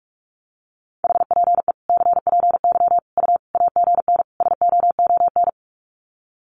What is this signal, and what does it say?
Morse code, heard as a keyed tone, about 44 wpm, 700 Hz. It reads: HPE CPY U AGN SOON
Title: Common QSO Phrases 45wpm